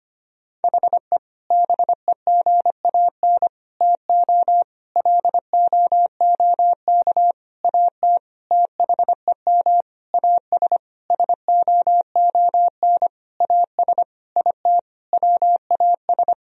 Morse code, heard as a keyed tone, about 25 wpm, 700 Hz. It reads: HE BEGAN TO LOOK AT THEM AS SOON AS IT WAS
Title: Sentences from Top 1000 Words - Review Part 1 - 25wpm